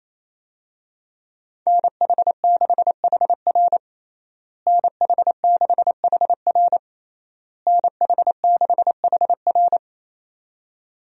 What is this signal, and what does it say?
Morse code, heard as a keyed tone, about 28 wpm, 700 Hz. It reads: NH6HR NH6HR NH6HR
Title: USA Call Signs - 5 Characters - Repeated 3x - 28wpm